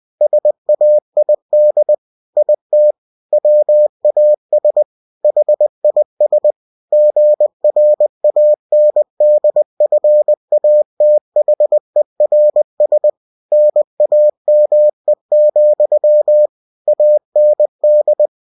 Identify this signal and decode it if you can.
Morse code, heard as a keyed tone, about 20 wpm, 600 Hz. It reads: SAID IT WAS HIS GRANDFATHERS NAME, AND
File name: cw-books / winnie-the-pooh / winnie-20wpm-page-0020